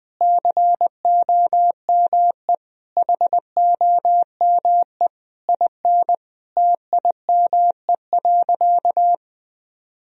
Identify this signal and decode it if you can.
Morse code, heard as a keyed tone, about 20 wpm, 700 Hz. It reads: COME HOME IN TIME.
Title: Sentences from Top 1000 Words - Review Part 3 - 20wpm